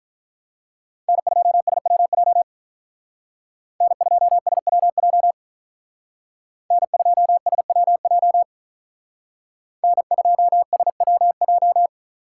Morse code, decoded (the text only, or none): N2SWJ N2SWJ N2SWJ N2SWJ